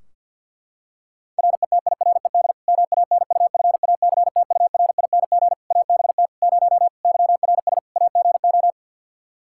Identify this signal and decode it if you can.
Morse code, heard as a keyed tone, about 48 wpm, 700 Hz. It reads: RETIRED MANUFACTURING ABT 0 YRS AGO
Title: Practice QSO - Exchange 3 - More Info - 50wpm